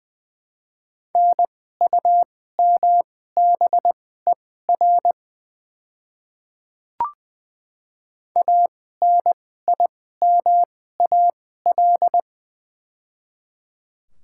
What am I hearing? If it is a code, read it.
Morse code, heard as a keyed tone, about 20 wpm, spaced as for 14 wpm, 700 Hz. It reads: NUMBER ANIMAL
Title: 2x Character Spacing - Top 300 Words - 20wpm